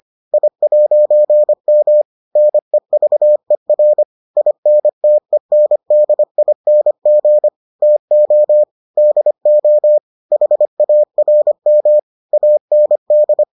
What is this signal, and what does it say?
Morse code, heard as a keyed tone, about 25 wpm, 600 Hz. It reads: I'M NEVER INTENDING TO DO HARM AND